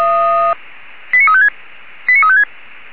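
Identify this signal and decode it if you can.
Five-tone selective call (ZVEI), heard as quick runs of five tones, tones of about 70 ms. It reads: 89356 89356
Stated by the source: ZVEI five-tone selective call received on SDR, fire brigade alerting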